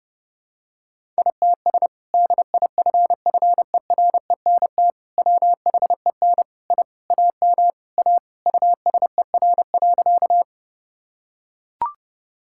Morse code, heard as a keyed tone, about 30 wpm, 700 Hz. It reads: ITS DIFFERENT WHEN I AM A USER.
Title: Sentences from Top 1000 Words - Review Part 5 - 30wpm